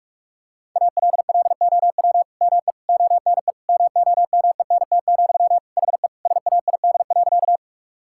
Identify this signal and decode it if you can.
Morse code, heard as a keyed tone, about 46 wpm, 700 Hz. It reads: ALLOW ME ONE MOMENT, HE SAID.
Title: Sentences from Top 1000 Words - Review Part 5 - 45wpm